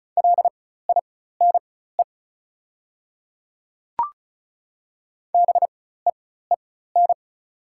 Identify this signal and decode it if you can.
Morse code, heard as a keyed tone, about 36 wpm, spaced as for 16 wpm, 700 Hz. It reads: LINE BEEN
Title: 4x Character Spacing - Top 200 Words - 35wpm